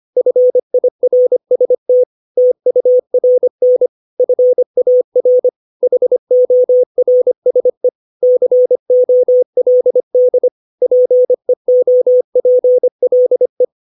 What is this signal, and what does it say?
Morse code, heard as a keyed tone, about 25 wpm, 500 Hz. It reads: FIRST TURN FAR HORSE COLD PEOPLE
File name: 632 words / 632 Words 25WPM